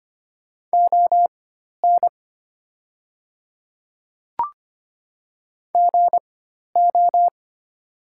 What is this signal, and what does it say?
Morse code, heard as a keyed tone, about 25 wpm, spaced as for 12 wpm, 700 Hz. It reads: ON GO
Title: 4x Character Spacing - Top 100 Words - 25wpm